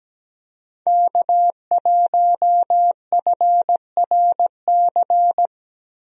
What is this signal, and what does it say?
Morse code, heard as a keyed tone, about 17 wpm, 700 Hz. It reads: K1FRC
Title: USA Call Signs - 5 Characters - Repeated 2x - 17wpm